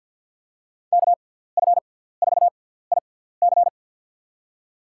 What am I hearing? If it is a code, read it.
Morse code, heard as a keyed tone, about 48 wpm, spaced as for 17 wpm, 700 Hz. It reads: KF4IC
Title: USA Call Signs - 6x Character Spacing - 50wpm